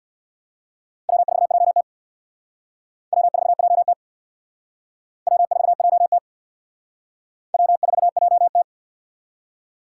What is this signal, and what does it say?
Morse code, heard as a keyed tone, about 59 wpm, 700 Hz. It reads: W4JT W4JT W4JT W4JT
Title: Speed Racing - USA 1x2 Call Signs 50wpm